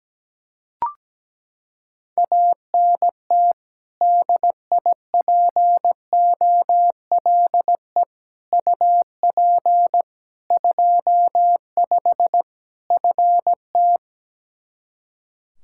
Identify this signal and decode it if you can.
Morse code, heard as a keyed tone, about 17 wpm, 700 Hz. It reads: ANT DIPOLE UP 25 FT
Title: Practice QSO - Exchange 2 - Info - 17wpm